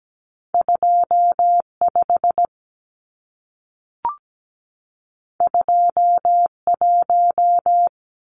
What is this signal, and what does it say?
Morse code, heard as a keyed tone, about 17 wpm, 700 Hz. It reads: 25 21
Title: Numbers - 2-Digit 17wpm